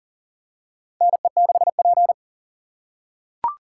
Morse code, heard as a keyed tone, about 40 wpm, 700 Hz. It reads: NE6P
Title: USA Call Signs - 3x Character Spacing - 40wpm